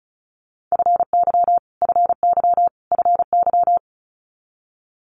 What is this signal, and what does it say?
Morse code, heard as a keyed tone, about 36 wpm, 700 Hz. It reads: FY FY FY
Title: Multi-letter Phonemes Repeated 3x 35wpm